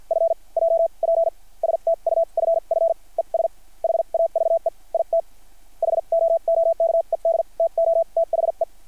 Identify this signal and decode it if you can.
Morse code, heard as a keyed tone, about 50 wpm, spaced as for 41 wpm, 660 Hz. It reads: F J P STUFF ES HAVE IT HOOKED TOTHE
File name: archived / CRSnet